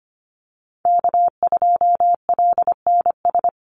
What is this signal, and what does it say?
Morse code, heard as a keyed tone, about 25 wpm, 700 Hz. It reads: K2LNS
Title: Call Signs Repeated 3x - 25wpm